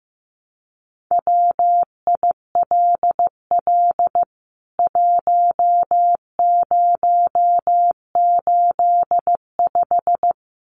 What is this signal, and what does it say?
Morse code, heard as a keyed tone, about 15 wpm, 700 Hz. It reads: WILL 1085
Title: CWT Exchange 15wpm